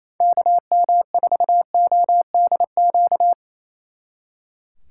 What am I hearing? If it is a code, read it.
Morse code, heard as a keyed tone, about 34 wpm, 700 Hz. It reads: KM4ODQ
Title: Speed Racing - USA 2x3 Call Signs 28wpm